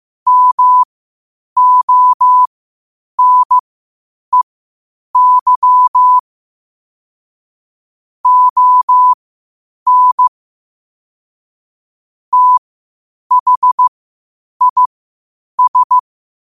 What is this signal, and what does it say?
Morse code, heard as a keyed tone, about 15 wpm, spaced as for 8 wpm, 1000 Hz. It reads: MONEY ON THIS
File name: qotdfiles / 8wpm